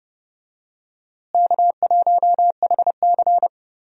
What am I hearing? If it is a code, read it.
Morse code, heard as a keyed tone, about 30 wpm, 700 Hz. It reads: K1HC
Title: USA Call Signs - 4 Characters - Repeated 2x - 30wpm